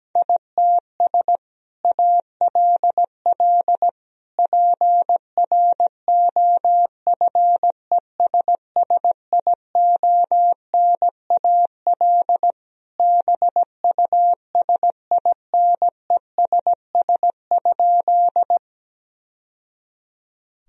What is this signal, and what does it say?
Morse code, heard as a keyed tone, about 17 wpm, 700 Hz. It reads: ITS ALL PROFESSIONAL BUSINESS?
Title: Sentences from Top 1000 Words - Review Part 4 - 17wpm